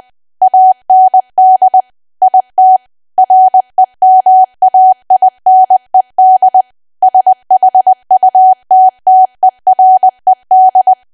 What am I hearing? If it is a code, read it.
Morse code, heard as a keyed tone, about 20 wpm, 750 Hz. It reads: AND IT REMAINED SHUTTERED
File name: cwa / W1AW-Level3 / 20WPM